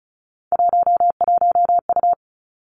Morse code, heard as a keyed tone, about 36 wpm, 700 Hz. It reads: 11U
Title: Three Challenging Letter-Number Combinations 35wpm